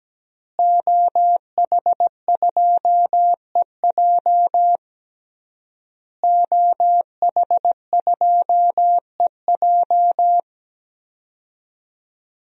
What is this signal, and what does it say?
Morse code, heard as a keyed tone, about 17 wpm, 700 Hz. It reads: OH2EJ OH2EJ
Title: DXCC-100 Call Signs - 5 Characters - Repeated 3x - 17wpm